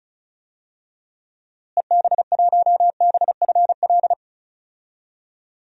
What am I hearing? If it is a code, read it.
Morse code, heard as a keyed tone, about 34 wpm, 700 Hz. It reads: EB1BFL
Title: DXCC-100 Call Signs - 6 Characters - Repeated 2x - 35wpm